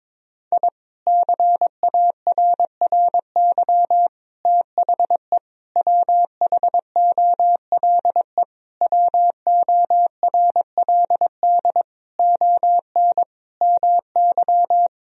Morse code, heard as a keyed tone, about 22 wpm, 700 Hz. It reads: I CARRY THE WHOLE WORLD ON MY
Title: Sentences from Top 1000 Words - Review Part 1 - 22wpm